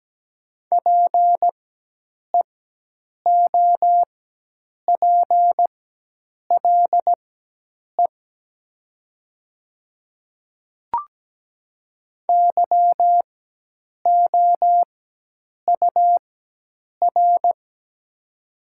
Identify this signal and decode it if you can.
Morse code, heard as a keyed tone, about 17 wpm, spaced as for 8 wpm, 700 Hz. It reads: PEOPLE YOUR
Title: 4x Character Spacing - Top 100 Words - 17wpm